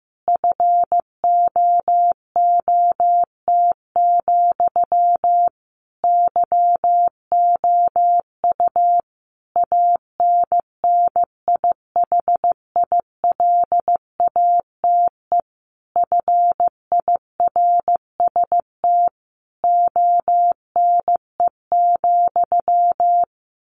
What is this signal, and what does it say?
Morse code, heard as a keyed tone, about 15 wpm, 700 Hz. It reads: FOOT, YOU ANNIHILATE FIRST ONE,
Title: A Sound Of Thunder 15wpm